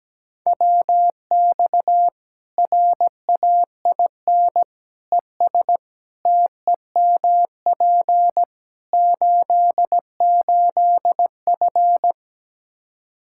Morse code, heard as a keyed tone, about 17 wpm, 700 Hz. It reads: WX RAIN ES TEMP 88F
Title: Practice QSO - Exchange 2 - Info - 17wpm